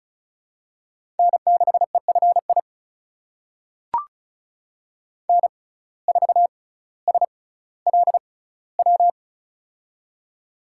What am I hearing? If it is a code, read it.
Morse code, heard as a keyed tone, about 36 wpm, spaced as for 18 wpm, 700 Hz. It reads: N6EFI N4SLW
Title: USA Call Signs - 5 Characters - 6x Character Spacing - 35wpm